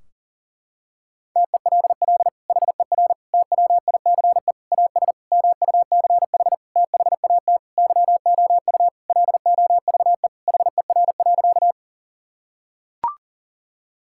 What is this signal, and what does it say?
Morse code, heard as a keyed tone, about 40 wpm, 700 Hz. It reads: TELL HER TWICE AS MUCH THAT YOU LOVE HER.
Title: Sentences from Top 1600 Words 40wpm